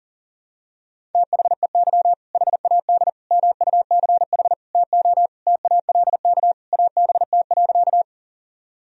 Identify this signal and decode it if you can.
Morse code, heard as a keyed tone, about 40 wpm, 700 Hz. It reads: THEY HAD MUCH TO TALK ABT.
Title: Sentences from Top 400 Words and Abbreviations 40wpm